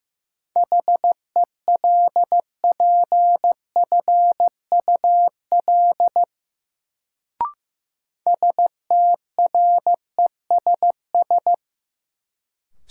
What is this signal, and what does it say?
Morse code, heard as a keyed tone, about 15 wpm, 700 Hz. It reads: HELPFUL STRESS
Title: Sentences from Top 1000 Words - Review Part 6 - New Word Drill - 15wpm